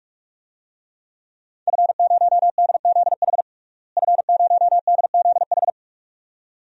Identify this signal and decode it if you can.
Morse code, heard as a keyed tone, about 47 wpm, 700 Hz. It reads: F0DZH F0DZH
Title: DXCC-100 Call Signs - 5 Characters - Repeated 2x - 45wpm